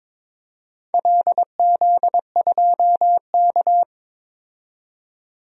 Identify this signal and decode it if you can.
Morse code, heard as a keyed tone, about 22 wpm, 700 Hz. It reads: LZ2K
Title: DXCC-100 Call Signs - 4 Characters - 22wpm